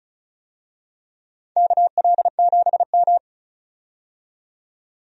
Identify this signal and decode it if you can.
Morse code, heard as a keyed tone, about 36 wpm, 700 Hz. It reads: KL7M
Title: DXCC-100 Call Signs - 4 Characters - 35wpm